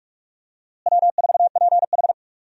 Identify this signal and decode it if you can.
Morse code, heard as a keyed tone, about 43 wpm, 700 Hz. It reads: W4PH
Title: USA Call Signs - 4 Characters - Repeated 2x - 45wpm